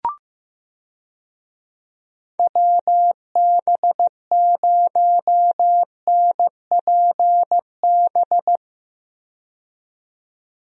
Morse code, heard as a keyed tone, about 15 wpm, 700 Hz. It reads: WB0NPB
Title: USA Call Signs - 6 Characters - 15wpm